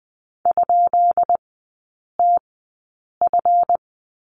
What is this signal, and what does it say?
Morse code, heard as a keyed tone, about 20 wpm, spaced as for 8 wpm, 700 Hz. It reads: ?TF
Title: 191 Lesson 31e - Letter v - Three Characters 20wpm